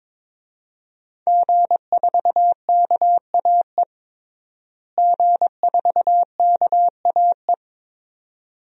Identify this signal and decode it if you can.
Morse code, heard as a keyed tone, about 22 wpm, 700 Hz. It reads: G4KAE G4KAE
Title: DXCC-100 Call Signs - 5 Characters - Repeated 2x - 22wpm